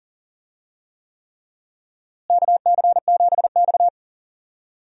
Morse code, heard as a keyed tone, about 40 wpm, 700 Hz. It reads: KC7X
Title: USA Call Signs - 4 Characters - Repeated 3x - 40wpm